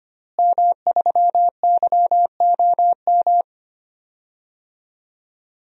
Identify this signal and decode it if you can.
Morse code, heard as a keyed tone, about 25 wpm, 700 Hz. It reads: M3YOM
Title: DXCC-100 Call Signs - 5 Characters - Repeated 3x - 25wpm